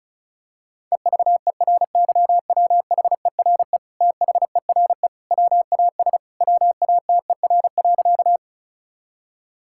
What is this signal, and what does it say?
Morse code, heard as a keyed tone, about 36 wpm, 700 Hz. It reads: EVERYWHERE THERE WAS WATER.
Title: Sentences from Top 1600 Words 35wpm